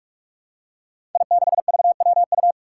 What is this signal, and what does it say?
Morse code, heard as a keyed tone, about 44 wpm, 700 Hz. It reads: I6VWU
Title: Five Challenging Letter-Number Combinations 45wpm